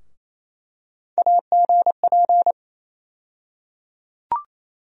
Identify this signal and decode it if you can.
Morse code, heard as a keyed tone, about 28 wpm, 700 Hz. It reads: AGP
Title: USA Call Sign Postfixes - 3 Letters 28wpm